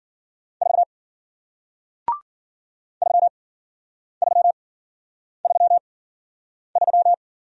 Morse code, heard as a keyed tone, about 54 wpm, spaced as for 9 wpm, 700 Hz. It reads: 4 3333